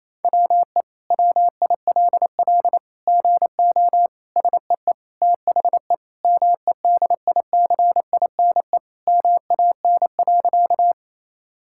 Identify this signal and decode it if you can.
Morse code, heard as a keyed tone, about 28 wpm, 700 Hz. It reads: WE WILL GO SEE THE MEDICINE MAN.
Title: Sentences from Top 1000 Words - Review Part 6 - 28wpm